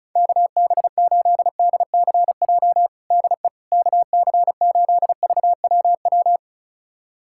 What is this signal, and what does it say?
Morse code, heard as a keyed tone, about 34 wpm, 700 Hz. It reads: KB8DCJ DE KC8VWW
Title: USA Callsign DE Callsign - 6 Characters - 35wpm